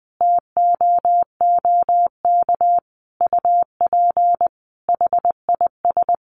Morse code, heard as a keyed tone, about 20 wpm, 700 Hz. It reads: TOOK UP HIS
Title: Sentences from Top 1500 Words 20wpm